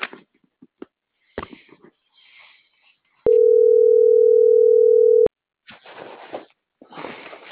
Ringback tone (North America), 440 + 480 Hz.